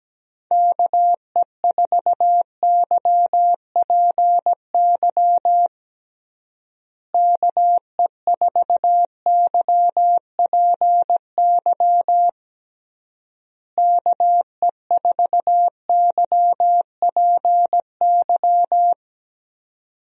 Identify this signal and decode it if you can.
Morse code, heard as a keyed tone, about 17 wpm, 700 Hz. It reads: KE4YPY KE4YPY KE4YPY
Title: USA Call Signs - 6 Characters - Repeated 3x - 17wpm